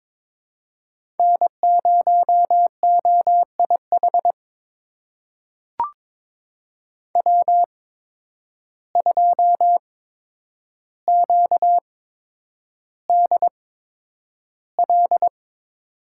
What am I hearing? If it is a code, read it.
Morse code, heard as a keyed tone, about 22 wpm, spaced as for 9 wpm, 700 Hz. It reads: N0OIH W2QDL
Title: USA Call Signs - 5 Characters - 8x Character Spacing - 22wpm